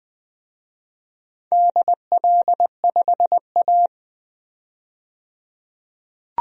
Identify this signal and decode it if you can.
Morse code, heard as a keyed tone, about 21 wpm, 700 Hz. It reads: DL5A
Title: DXCC-100 Call Signs - 4 Characters - 20wpm